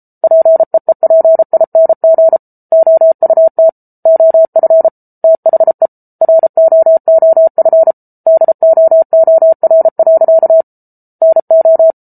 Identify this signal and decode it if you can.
Morse code, heard as a keyed tone, about 34 wpm, 650 Hz. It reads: PEEPING OUT OF THE ROOF DOOR. NO